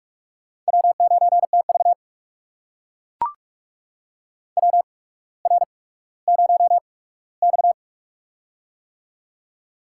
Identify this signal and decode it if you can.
Morse code, heard as a keyed tone, about 44 wpm, spaced as for 19 wpm, 700 Hz. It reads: W9TV WR0X
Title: USA Call Signs - 4 Characters - 8x Character Spacing - 45wpm